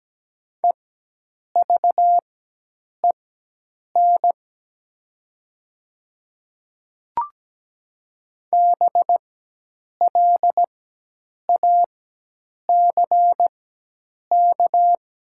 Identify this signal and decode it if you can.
Morse code, heard as a keyed tone, about 17 wpm, spaced as for 8 wpm, 700 Hz. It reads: EVEN BLACK
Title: 4x Character Spacing - Top 500 Words - 17wpm